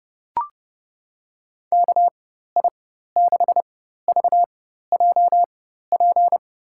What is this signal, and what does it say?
Morse code, heard as a keyed tone, about 31 wpm, spaced as for 14 wpm, 700 Hz. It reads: KI6VJP
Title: USA Call Signs - 4x Character Spacing - 30wpm